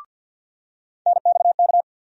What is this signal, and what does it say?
Morse code, heard as a keyed tone, about 46 wpm, 700 Hz. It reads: NXK
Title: USA Call Sign Postfixes - 3 Letters 50wpm